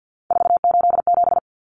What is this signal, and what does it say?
Morse code, heard as a keyed tone, about 50 wpm, 700 Hz. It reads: 487
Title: Common Cut Numbers - 3-Digit 50wpm